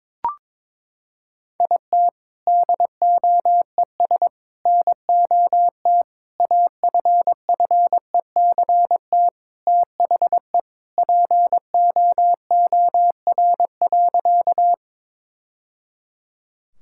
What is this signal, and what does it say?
Morse code, heard as a keyed tone, about 22 wpm, 700 Hz. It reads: IT DOES NOT AFFECT THE POOR.
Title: Sentences from Top 1000 Words - Review Part 6 - 22wpm